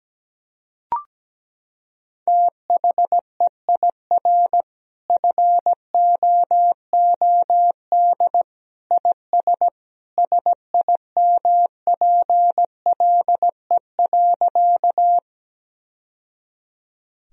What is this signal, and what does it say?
Morse code, heard as a keyed tone, about 17 wpm, 700 Hz. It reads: THEIR FOOD IS SIMPLE.
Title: Sentences from Top 1000 Words - Review Part 1 - 17wpm